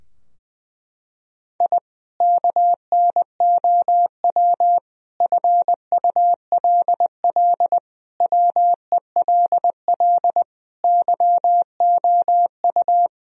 Morse code, heard as a keyed tone, about 20 wpm, 700 Hz. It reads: I KNOW FULL WELL YOU
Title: Sentences from Top 1000 Words - Review Part 3 - 20wpm